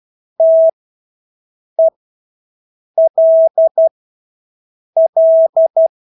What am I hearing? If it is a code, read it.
Morse code, heard as a keyed tone, about 12 wpm, spaced as for 6 wpm, 650 Hz. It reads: TELL
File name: Engelske ord 12-6 wpm